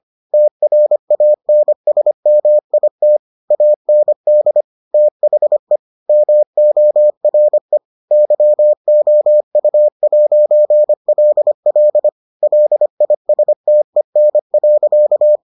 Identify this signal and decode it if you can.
Morse code, heard as a keyed tone, about 25 wpm, 600 Hz. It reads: TRANSMIT AND THE MORE YOU'LL LISTEN.